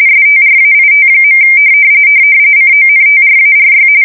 RTTY, about 45.45 baud. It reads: STRAIT. CONTRARY
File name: fsk170 / battleshipnj